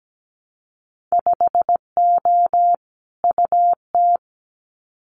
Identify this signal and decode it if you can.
Morse code, heard as a keyed tone, about 17 wpm, 700 Hz. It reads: 5O UT
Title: Winter Field Day Exchanges 17wpm